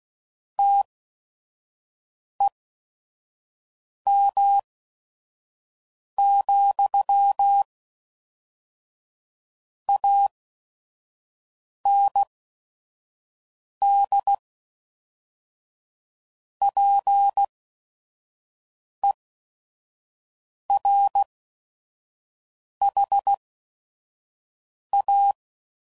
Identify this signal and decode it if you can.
Morse code, heard as a keyed tone, about 16 wpm, spaced as for 5 wpm, 800 Hz. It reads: TEM, AND PERHA
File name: CW Code Groups @5WPM